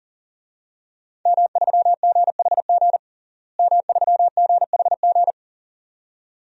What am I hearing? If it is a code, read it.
Morse code, heard as a keyed tone, about 40 wpm, 690 Hz. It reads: M3GHG M3GHG
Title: DXCC-100 Call Signs - 5 Characters - Repeated 2x - 40wpm